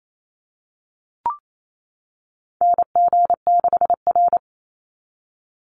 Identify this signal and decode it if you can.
Morse code, heard as a keyed tone, about 28 wpm, 700 Hz. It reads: NG6R
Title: NA SOTA Chaser Callsigns 28wpm